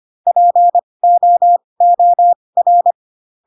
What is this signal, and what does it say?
Morse code, heard as a keyed tone, about 25 wpm, spaced as for 20 wpm, 700 Hz. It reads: POOR
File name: Pinocchio / Pinocchio ch2 15 25 ik1hge